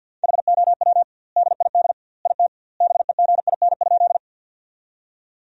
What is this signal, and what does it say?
Morse code, heard as a keyed tone, about 50 wpm, 700 Hz. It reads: HOW DID IT BEGIN?
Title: 2x Word-Spacing - Sentences from Top 300 Words - 50wpm